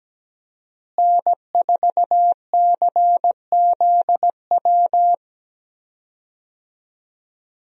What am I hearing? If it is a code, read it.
Morse code, heard as a keyed tone, about 17 wpm, 700 Hz. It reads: N4CZW
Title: USA Call Signs - 5 Characters - Repeated 2x - 17wpm